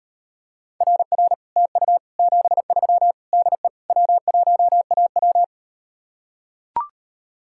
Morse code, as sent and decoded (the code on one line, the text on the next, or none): .-. .-. / - ..- / --... ...-- / -.. / .-- .---- .- .--
RR TU 73 DE W1AW